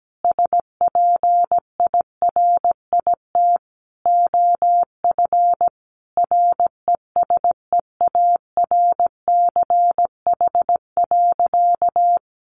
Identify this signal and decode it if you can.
Morse code, heard as a keyed tone, about 17 wpm, 700 Hz. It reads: SPIRIT OF RESEARCH.
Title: Sentences from Top 1400 Words 17wpm